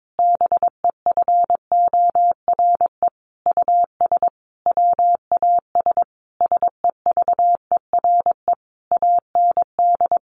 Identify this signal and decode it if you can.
Morse code, heard as a keyed tone, about 22 wpm, 700 Hz. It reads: BEFORE US WAS SEVERE AND
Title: Sentences from Top 1400 Words 22wpm